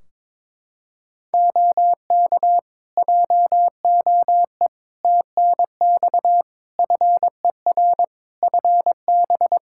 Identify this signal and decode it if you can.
Morse code, heard as a keyed tone, about 22 wpm, 700 Hz. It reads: OK JOE TNX FER FB
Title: K1USN SST QSO Practice 22wpm